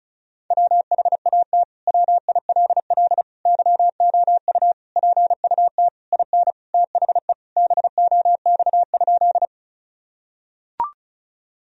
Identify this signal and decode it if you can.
Morse code, heard as a keyed tone, about 36 wpm, 700 Hz. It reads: WHAT WILL YOU PUT IN THE BOX?
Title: Sentences from Top 1000 Words - Review Part 2 - 35wpm